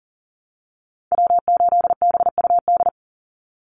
Morse code, heard as a keyed tone, about 40 wpm, 700 Hz. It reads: W8BUD
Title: Call Signs Repeated 2x - 40wpm